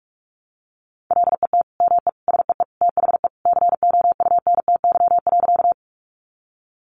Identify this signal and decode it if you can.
Morse code, heard as a keyed tone, about 44 wpm, 700 Hz. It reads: LET ME SEE THE COUNTY.
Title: Sentences from Top 1200 Words 45wpm